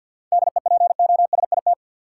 Morse code, heard as a keyed tone, about 53 wpm, 700 Hz. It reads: DEPOSIT
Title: Sentences from Top 1300 Words - New Word Drill - 50wpm